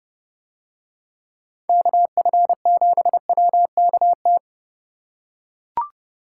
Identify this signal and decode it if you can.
Morse code, heard as a keyed tone, about 30 wpm, 700 Hz. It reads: KF7WKT